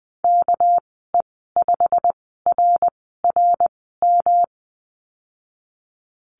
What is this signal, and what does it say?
Morse code, heard as a keyed tone, about 20 wpm, spaced as for 14 wpm, 700 Hz. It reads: KE5RRM
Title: 200 Lesson 32g - Letter k - Call Signs 20wpm